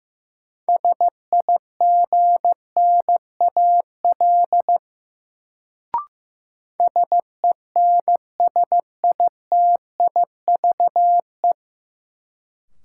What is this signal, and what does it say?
Morse code, heard as a keyed tone, about 15 wpm, 700 Hz. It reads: SIGNAL SENSITIVE
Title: Sentences from Top 1400 Words - New Word Drill - 15wpm